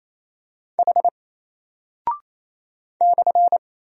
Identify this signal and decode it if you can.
Morse code, heard as a keyed tone, about 27 wpm, 700 Hz. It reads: H /
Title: Single Character (Letter:Number:Punctuation:Prosign) 28wpm